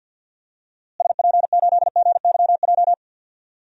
Morse code, heard as a keyed tone, about 50 wpm, 700 Hz. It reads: SP8GYJ
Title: DXCC-100 Call Signs - 6 Characters - Repeated 2x - 50wpm